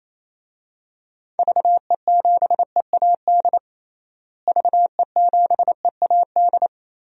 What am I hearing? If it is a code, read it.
Morse code, heard as a keyed tone, about 28 wpm, 700 Hz. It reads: VE7EAD VE7EAD
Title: DXCC-100 Call Signs - 6 Characters - Repeated 3x - 28wpm